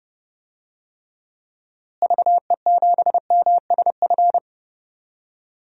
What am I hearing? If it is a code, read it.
Morse code, heard as a keyed tone, about 30 wpm, 700 Hz. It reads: VE7MSF
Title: DXCC-100 Call Signs - 6 Characters - Repeated 2x - 30wpm